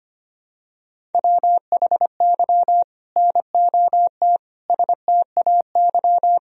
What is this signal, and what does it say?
Morse code, heard as a keyed tone, about 25 wpm, 700 Hz. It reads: WHY NOT STAY
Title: Sentences from Top 1000 Words - Review Part 1 - 25wpm